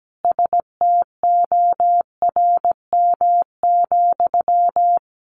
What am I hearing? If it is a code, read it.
Morse code, heard as a keyed tone, about 17 wpm, 700 Hz. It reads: STORM,
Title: Sentences from Top 1500 Words 17wpm